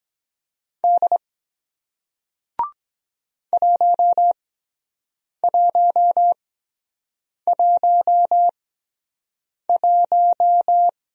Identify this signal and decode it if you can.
Morse code, heard as a keyed tone, about 24 wpm, spaced as for 7 wpm, 700 Hz. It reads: D 1111